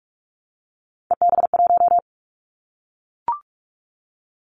Morse code, heard as a keyed tone, about 44 wpm, 700 Hz. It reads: EB1
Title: Callsign Prefixes for Top 25 DXCC - 45wpm